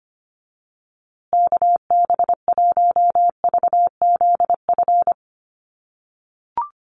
Morse code, heard as a keyed tone, about 25 wpm, 700 Hz. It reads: KB1VZF
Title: USA Call Signs Repeated 2x 25wpm